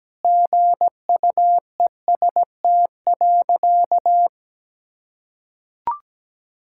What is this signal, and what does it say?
Morse code, heard as a keyed tone, about 17 wpm, 700 Hz. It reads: GUEST.
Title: Sentences from Top 1600 Words 17wpm